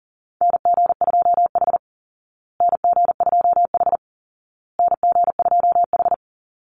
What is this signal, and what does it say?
Morse code, heard as a keyed tone, about 37 wpm, 700 Hz. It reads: NG2H NG2H NG2H
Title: Call Signs Repeated 2x - 40wpm